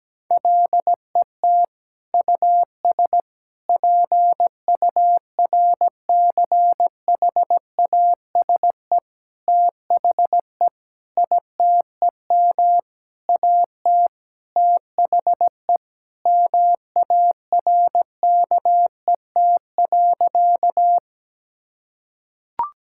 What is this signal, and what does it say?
Morse code, heard as a keyed tone, about 17 wpm, 700 Hz. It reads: LET US PURCHASE THE ITEM AT THE MARKET.
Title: Sentences from Top 1000 Words - Review Part 6 - 17wpm